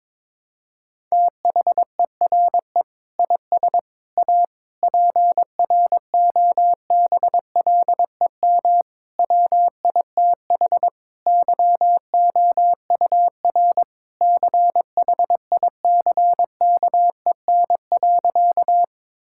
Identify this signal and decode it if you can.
Morse code, heard as a keyed tone, about 22 wpm, 700 Hz. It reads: THERE IS A PROBLEM WITH YOUR CHICKEN.
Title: Sentences from Top 1000 Words - Review Part 5 - 22wpm